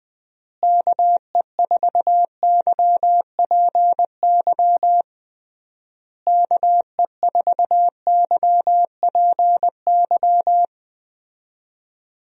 Morse code, heard as a keyed tone, about 20 wpm, 700 Hz. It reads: KE4YPY KE4YPY
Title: USA Call Signs - 6 Characters - Repeated 3x - 20wpm